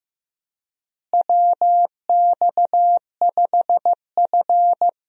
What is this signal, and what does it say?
Morse code, heard as a keyed tone, about 15 wpm, 700 Hz. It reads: WX5F
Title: USA Call Signs - 4 Characters - Repeated 3x - 15wpm